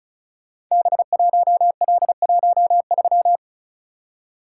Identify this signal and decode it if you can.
Morse code, heard as a keyed tone, about 36 wpm, 690 Hz. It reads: B1L13
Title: Code Groups - 5 Letter:Numbers - 35wpm